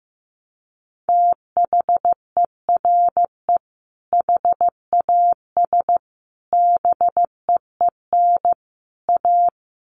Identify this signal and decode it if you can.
Morse code, heard as a keyed tone, about 15 wpm, 700 Hz. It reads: THERE HAS BEEN A
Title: Sets of 4 Words 15wpm